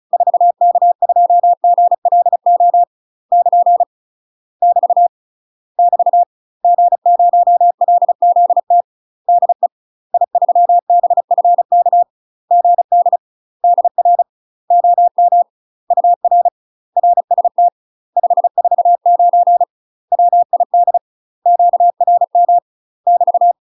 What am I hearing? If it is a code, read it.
Morse code, heard as a keyed tone, about 36 wpm, 700 Hz. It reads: 4K2GLO <KN> <BT> <BT> G0LZT DE I3BFK GD DR OM UR RST 549 WID QRM <BT>
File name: UK MASTER QSOs TRAINING / 35WPM / UK MASTER QSOs 0023